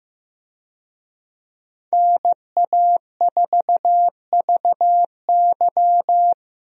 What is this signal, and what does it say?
Morse code, heard as a keyed tone, about 15 wpm, 700 Hz. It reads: NA4VY
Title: USA Call Signs - 5 Characters - Repeated 2x - 15wpm